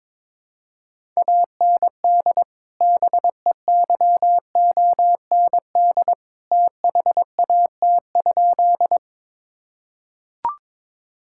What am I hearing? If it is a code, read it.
Morse code, heard as a keyed tone, about 22 wpm, 700 Hz. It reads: AND BEYOND THAT?
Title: Sentences from Top 1000 Words - Review Part 6 - 22wpm